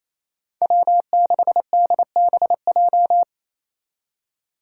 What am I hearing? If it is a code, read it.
Morse code, heard as a keyed tone, about 34 wpm, 700 Hz. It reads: W6DBJ
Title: Speed Racing - USA 1x3 Call Signs 28wpm